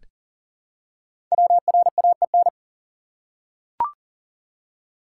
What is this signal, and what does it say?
Morse code, heard as a keyed tone, about 40 wpm, 700 Hz. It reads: WRAEN
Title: Code Groups - 5 Letters - 40wpm